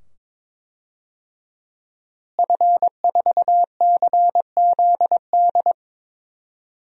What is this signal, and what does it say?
Morse code, heard as a keyed tone, about 22 wpm, 700 Hz. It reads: F4CZD
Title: DXCC-100 Call Signs - 5 Characters - Repeated 2x - 22wpm